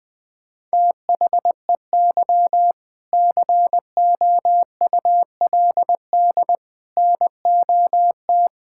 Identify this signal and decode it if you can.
Morse code, heard as a keyed tone, about 21 wpm, 700 Hz. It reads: THEY COULD NOT
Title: Sentences with 7-Word Phrases - Spoken Beginning - 20wpm